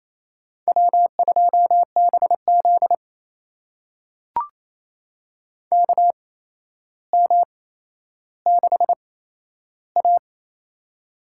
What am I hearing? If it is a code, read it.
Morse code, heard as a keyed tone, about 28 wpm, spaced as for 12 wpm, 700 Hz. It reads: W2BZ KM6A
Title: USA Call Signs - 4 Characters - 8x Character Spacing - 28wpm